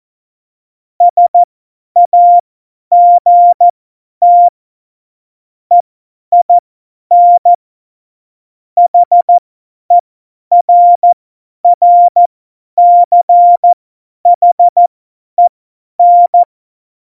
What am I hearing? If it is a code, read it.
Morse code, heard as a keyed tone, about 14 wpm, spaced as for 10 wpm, 700 Hz. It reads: SAGT EIN HERRCHEN
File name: Morsekurs / Witze / Witz 25. Dezember, 50 F70